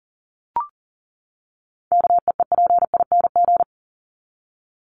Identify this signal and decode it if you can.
Morse code, heard as a keyed tone, about 40 wpm, 700 Hz. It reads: KEEPING
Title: ING Suffix - Top 100 Words with Base Words 40wpm